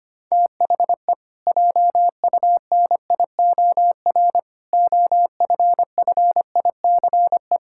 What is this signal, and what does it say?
Morse code, heard as a keyed tone, about 25 wpm, 700 Hz. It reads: THE JUNIOR OFFICE
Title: Pangram Challenge v2 - 25wpm